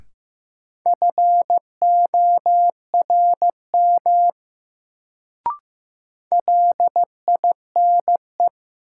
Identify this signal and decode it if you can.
Morse code, heard as a keyed tone, about 15 wpm, 700 Hz. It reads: FORM LINE
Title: Sentences from Top 1000 Words - Review Part 1 - New Word Drill - 15wpm